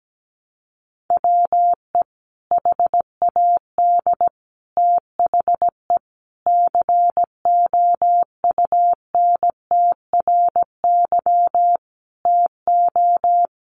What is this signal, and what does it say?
Morse code, heard as a keyed tone, about 17 wpm, 700 Hz. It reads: WE HAD THE COUNTRY TO
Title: Sentences from Top 1100 Words 17wpm